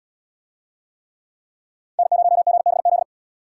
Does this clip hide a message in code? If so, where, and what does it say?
Yes: Morse code from 2.0 to 3.0 s, heard as a keyed tone, about 77 wpm, 700 Hz. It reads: N0GBZ